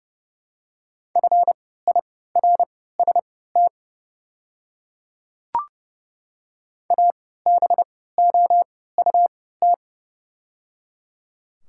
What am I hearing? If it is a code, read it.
Morse code, heard as a keyed tone, about 30 wpm, spaced as for 17 wpm, 700 Hz. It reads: FIRST ABOUT
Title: 3x Character Spacing - Top 100 Words - 30wpm